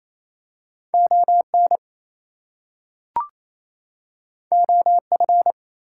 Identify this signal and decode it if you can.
Morse code, heard as a keyed tone, about 28 wpm, 700 Hz. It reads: ON OF
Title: Two Letter Words - 28wpm